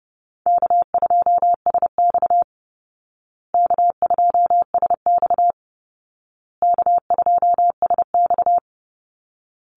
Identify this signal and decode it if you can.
Morse code, heard as a keyed tone, about 30 wpm, 700 Hz. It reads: K2SX K2SX K2SX
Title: Call Signs Repeated 2x - 30wpm